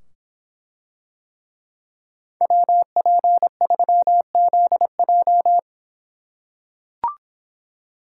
Morse code, heard as a keyed tone, about 25 wpm, 700 Hz. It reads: WP3ZJ